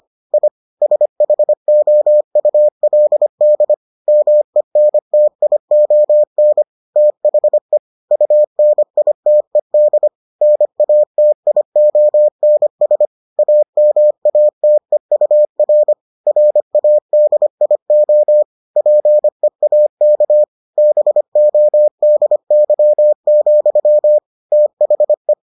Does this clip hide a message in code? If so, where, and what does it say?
Yes: Morse code from 0.3 to 25.3 s, heard as a keyed tone, about 25 wpm, 600 Hz. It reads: I SHOULD MENTION THE UNITED NATIONS AMATEUR RADIO PEAK BODY, THE